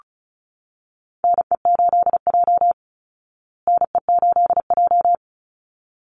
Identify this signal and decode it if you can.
Morse code, heard as a keyed tone, about 34 wpm, 700 Hz. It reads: NE8J NE8J
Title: Call Signs Repeated 3x - 35wpm